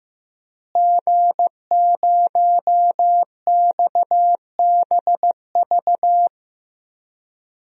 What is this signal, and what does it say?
Morse code, heard as a keyed tone, about 15 wpm, 700 Hz. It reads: G0XBV
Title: DXCC-100 Call Signs - 5 Characters - Repeated 3x - 15wpm